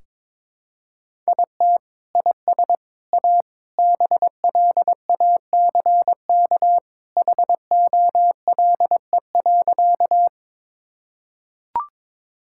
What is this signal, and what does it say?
Morse code, heard as a keyed tone, about 22 wpm, 700 Hz. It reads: IT IS A BLACK HOLE.
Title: Sentences from Top 1600 Words 22wpm